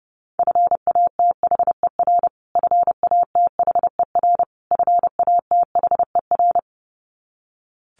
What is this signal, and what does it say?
Morse code, heard as a keyed tone, about 30 wpm, 700 Hz. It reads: FATHER FATHER FATHER
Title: Top 200 Words Repeated 30wpm